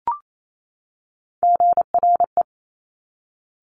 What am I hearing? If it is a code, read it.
Morse code, heard as a keyed tone, about 28 wpm, 700 Hz. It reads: GRE
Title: Top Three Letter Combinations in Words 28wpm